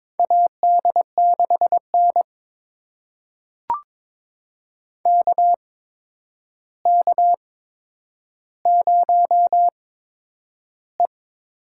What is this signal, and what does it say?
Morse code, heard as a keyed tone, about 22 wpm, spaced as for 9 wpm, 700 Hz. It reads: AD6N KK0E
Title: USA Call Signs - 4 Characters - 8x Character Spacing - 22wpm